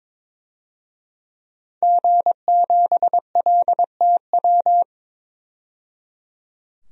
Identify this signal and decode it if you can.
Morse code, heard as a keyed tone, about 22 wpm, 700 Hz. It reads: G7LTW
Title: DXCC-100 Call Signs - 5 Characters - 22wpm